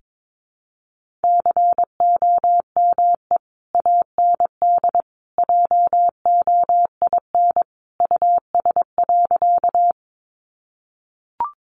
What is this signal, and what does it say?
Morse code, heard as a keyed tone, about 22 wpm, 700 Hz. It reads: COME AND JOIN US.
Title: Sentences from Top 1400 Words 22wpm